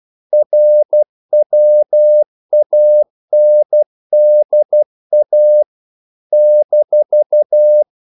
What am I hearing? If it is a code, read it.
Morse code, heard as a keyed tone, about 12 wpm, 600 Hz. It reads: RWANDA -
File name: CW / Practice / NationalCapitals12